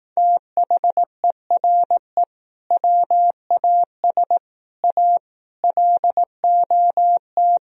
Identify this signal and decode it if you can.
Morse code, heard as a keyed tone, about 18 wpm, 700 Hz. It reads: THERE WAS A LOT
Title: Speed-Racing - Sets of 5 Words 15wpm